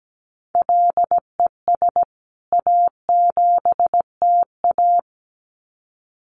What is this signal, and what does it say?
Morse code, heard as a keyed tone, about 17 wpm, 700 Hz. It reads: LES A7TA
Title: CWT Exchange 17wpm